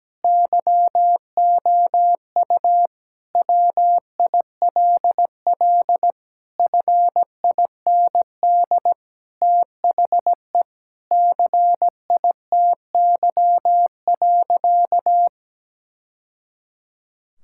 Morse code, heard as a keyed tone, about 17 wpm, 700 Hz. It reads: YOU WILL FIND THE CITY.
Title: Sentences from Top 1000 Words - Review Part 6 - 17wpm